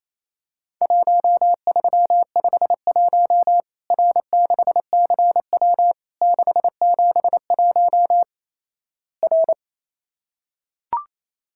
Morse code, heard as a keyed tone, about 28 wpm, 700 Hz. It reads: 1351 R6CW 671 R
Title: Worked All Europe DX Contest - QTC Exchange - 28wpm